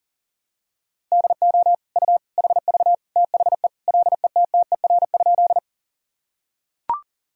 Morse code, heard as a keyed tone, about 37 wpm, 700 Hz. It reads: DO U HV THE LETTER?
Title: Sentences from Top 300 Words and Abbreviations 40wpm